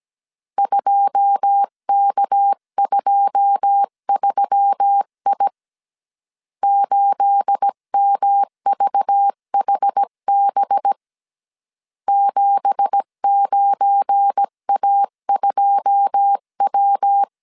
Morse code, heard as a keyed tone, about 17 wpm, 800 Hz. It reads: 2K23I 8MVHB 79A2W
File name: CW / DET / 1 9 1500C S B 16wpm 2856